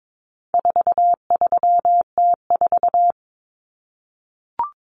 Morse code, heard as a keyed tone, about 22 wpm, 700 Hz. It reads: 43T4
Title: Common Cut Numbers - 4-Digit 22wpm